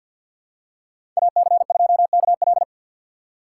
Morse code, heard as a keyed tone, about 53 wpm, 700 Hz. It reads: AC2KL
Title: USA Call Signs - 5 Characters - 6x Character Spacing - 50wpm